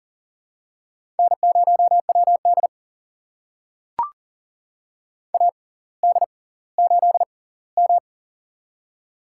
Morse code, heard as a keyed tone, about 39 wpm, spaced as for 20 wpm, 700 Hz. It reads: N0WD AD8M
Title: USA Call Signs - 4 Characters - 6x Character Spacing - 40wpm